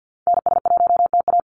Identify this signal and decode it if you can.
Morse code, heard as a keyed tone, about 48 wpm, 700 Hz. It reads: NS1TA
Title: NA SOTA Activator Callsigns 50wpm